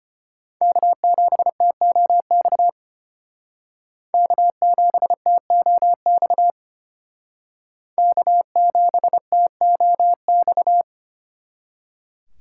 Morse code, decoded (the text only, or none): K7TOX K7TOX K7TOX